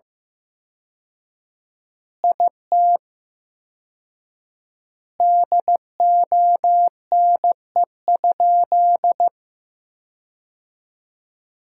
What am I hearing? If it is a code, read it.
Morse code, heard as a keyed tone, about 15 wpm, 700 Hz. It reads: IT DONE?
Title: 4x Word-Spacing - Sentences from Top 500 Words - 15wpm